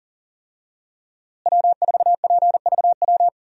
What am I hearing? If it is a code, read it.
Morse code, heard as a keyed tone, about 40 wpm, 700 Hz. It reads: W4PVW
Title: USA Call Signs - 5 Characters - Repeated 3x - 40wpm